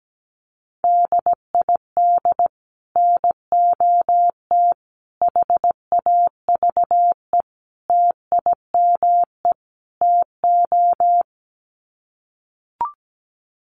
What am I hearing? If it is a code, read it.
Morse code, heard as a keyed tone, about 17 wpm, 700 Hz. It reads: DID NOT HAVE TIME TO
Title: Sets of 5 Words Encore 17wpm